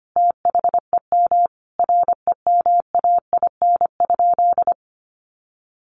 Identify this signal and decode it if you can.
Morse code, heard as a keyed tone, about 25 wpm, 700 Hz. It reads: THEM REMAIN?
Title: Sentences from Top 1200 Words 25wpm